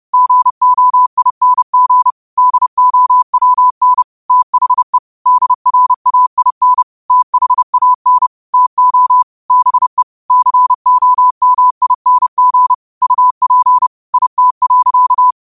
Morse code, heard as a keyed tone, about 30 wpm, 1000 Hz. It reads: GOING DOWN THE DRAIN THAN TO BE COMING UP IT.
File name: qotdfiles / 30wpm